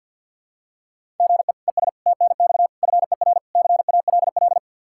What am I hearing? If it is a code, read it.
Morse code, heard as a keyed tone, about 51 wpm, 700 Hz. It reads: GE ES TNX FER CALL
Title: K1USN SST QSO Practice 50wpm